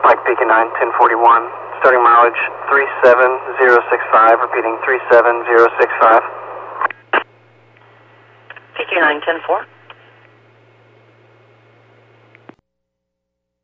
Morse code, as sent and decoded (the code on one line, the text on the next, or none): none